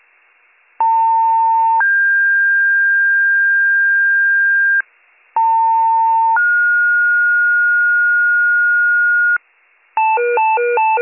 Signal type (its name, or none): two-tone page